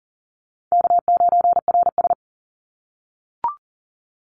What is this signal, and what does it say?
Morse code, heard as a keyed tone, about 40 wpm, 700 Hz. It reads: K9RS
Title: Easy Call Signs - 40wpm